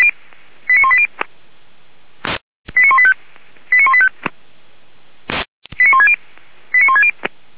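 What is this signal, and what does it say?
Five-tone selective call (ZVEI), heard as quick runs of five tones, tones of about 80 ms. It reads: 89180 89185 89185 89160 89160
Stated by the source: ZVEI five-tone selective call received by SDR.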